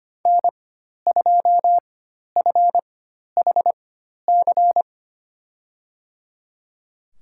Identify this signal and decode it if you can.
Morse code, heard as a keyed tone, about 25 wpm, spaced as for 12 wpm, 700 Hz. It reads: N2FHC
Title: USA Call Signs - 4x Character Spacing - 25wpm